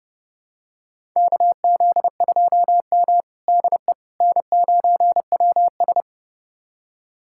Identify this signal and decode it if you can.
Morse code, heard as a keyed tone, about 30 wpm, 700 Hz. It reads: KZ2M DE N9WS
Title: USA Callsign DE Callsign - 4 Characters - 30wpm